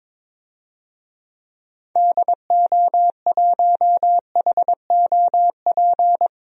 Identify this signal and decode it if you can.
Morse code, heard as a keyed tone, about 22 wpm, 700 Hz. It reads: DO1HOP
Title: DXCC-100 Call Signs - 6 Characters - 22wpm